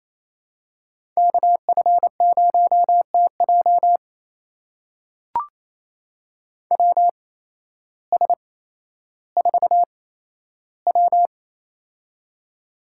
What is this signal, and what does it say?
Morse code, heard as a keyed tone, about 28 wpm, spaced as for 13 wpm, 700 Hz. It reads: KF0TJ WS4W
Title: USA Call Signs - 8x Character Spacing - 28wpm